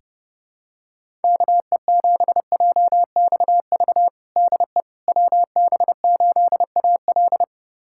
Morse code, heard as a keyed tone, about 30 wpm, 700 Hz. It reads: KE7JXV DE WB8AL
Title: USA Callsign DE Callsign 30wpm